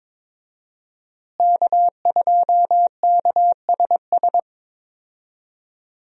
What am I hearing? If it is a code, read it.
Morse code, heard as a keyed tone, about 22 wpm, 700 Hz. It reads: K2KSS
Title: USA Call Signs - 5 Characters - Repeated 2x - 22wpm